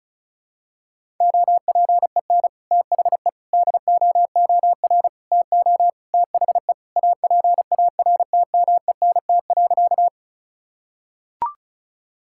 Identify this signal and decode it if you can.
Morse code, heard as a keyed tone, about 36 wpm, 700 Hz. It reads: OPEN THE DOOR TO THE APARTMENT.
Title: Sentences from Top 1000 Words - Review Part 6 - 35wpm